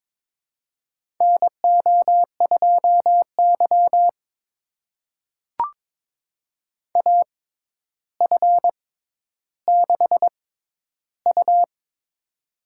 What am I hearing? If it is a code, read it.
Morse code, heard as a keyed tone, about 22 wpm, spaced as for 11 wpm, 700 Hz. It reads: NO2Y AF6U
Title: USA Call Signs - 4 Characters - 6x Character Spacing - 22wpm